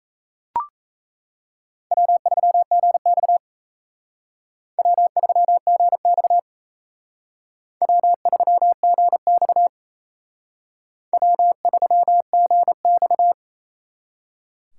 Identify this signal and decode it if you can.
Morse code, heard as a keyed tone, about 36 wpm, 700 Hz. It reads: W3GX W3GX W3GX W3GX